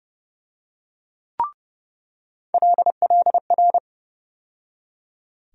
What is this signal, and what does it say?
Morse code, heard as a keyed tone, about 32 wpm, 700 Hz. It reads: LLR
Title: USA Call Sign Postfixes - 3 Letters 30wpm